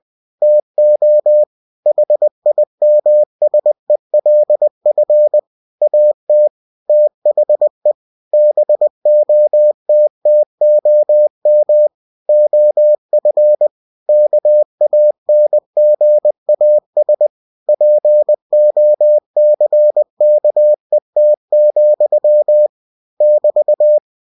Morse code, heard as a keyed tone, about 20 wpm, 600 Hz. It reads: TO HIMSELF AT THE BOTTOM OF KANGAS POCKET, <BT>
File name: cw-books / winnie-the-pooh / winnie-20wpm-page-0070